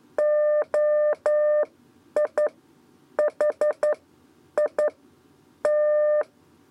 Morse code, heard as a keyed tone, about 11 wpm, spaced as for 8 wpm, 600 Hz. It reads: OIHI T
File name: Lezione trasmessa 2 lez 2 exe